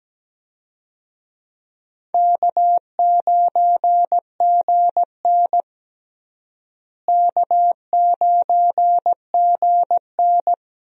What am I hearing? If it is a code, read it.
Morse code, heard as a keyed tone, about 17 wpm, 700 Hz. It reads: K9GN K9GN
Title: USA Call Signs - 4 Characters - Repeated 3x - 17wpm